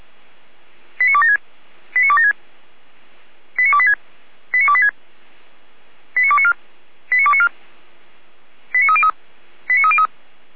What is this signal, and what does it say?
Five-tone selective call (ZVEI), heard as quick runs of five tones, tones of about 70 ms. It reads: 89276 89276 89286 89286 89294 89294 89303 89303